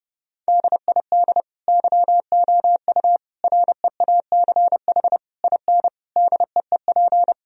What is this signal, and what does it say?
Morse code, heard as a keyed tone, about 30 wpm, 700 Hz. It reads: DID YOU REACH IN DEEP
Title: Sentences from Top 1000 Words - Review Part 3 - 30wpm